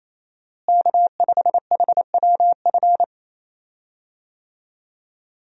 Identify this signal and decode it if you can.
Morse code, heard as a keyed tone, about 28 wpm, 700 Hz. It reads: K5HWF
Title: USA Call Signs - 5 Characters - Repeated 3x - 28wpm